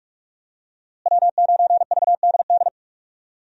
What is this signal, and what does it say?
Morse code, heard as a keyed tone, about 45 wpm, 700 Hz. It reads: W9VDD
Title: USA Call Signs - 5 Characters - Repeated 2x - 45wpm